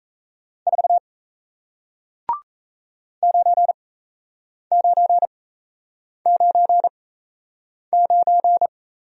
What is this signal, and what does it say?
Morse code, heard as a keyed tone, about 38 wpm, spaced as for 8 wpm, 700 Hz. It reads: 4 9999